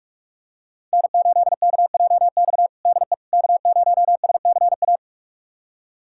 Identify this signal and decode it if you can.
Morse code, heard as a keyed tone, about 42 wpm, 700 Hz. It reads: N8KJX DE K0SCA
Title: USA Callsign DE Callsign - 5 Characters - 45wpm